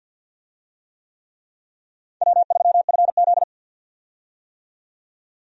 Morse code, heard as a keyed tone, about 53 wpm, 700 Hz. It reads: W3FZ
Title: USA Call Signs - 4 Characters - 50wpm